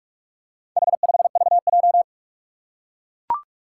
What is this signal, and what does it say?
Morse code, heard as a keyed tone, about 44 wpm, 700 Hz. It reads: H5VJ
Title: Four Challenging Letter-Number Combinations 45wpm